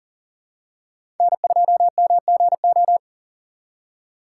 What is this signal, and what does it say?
Morse code, heard as a keyed tone, about 38 wpm, 700 Hz. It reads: N2MGO
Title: USA Call Signs - 5 Characters - 3x Character Spacing - 40wpm